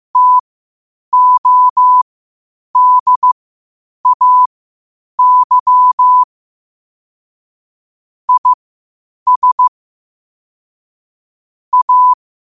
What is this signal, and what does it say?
Morse code, heard as a keyed tone, about 15 wpm, spaced as for 8 wpm, 1000 Hz. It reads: TODAY IS A
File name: qotdfiles / 8wpm